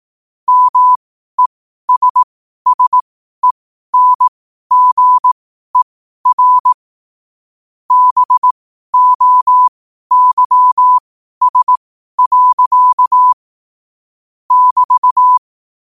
Morse code, heard as a keyed tone, about 18 wpm, spaced as for 12 wpm, 1000 Hz. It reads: MESSENGER BOYS. <BT>
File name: qotdfiles / 12wpm